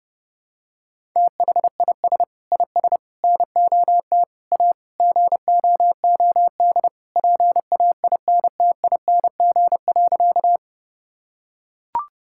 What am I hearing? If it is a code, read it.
Morse code, heard as a keyed tone, about 30 wpm, 700 Hz. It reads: THIS IS NOT A GOOD PAINTING.
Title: Sentences from Top 1000 Words - Review Part 6 - 30wpm